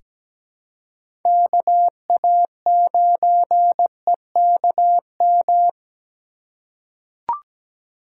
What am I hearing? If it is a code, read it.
Morse code, heard as a keyed tone, about 17 wpm, 700 Hz. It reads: KA9EKM
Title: USA Call Signs - 3x Character Spacing - 17wpm